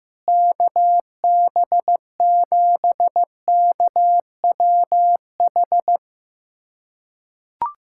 Morse code, heard as a keyed tone, about 15 wpm, 700 Hz. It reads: KB7KWH
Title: USA Callsign DE Callsign - 6 Characters - 15wpm